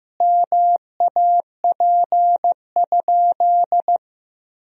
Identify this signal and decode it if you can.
Morse code, heard as a keyed tone, about 16 wpm, 700 Hz. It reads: MAP?
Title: Sentences from Top 1000 Words - Review Part 2 - 15wpm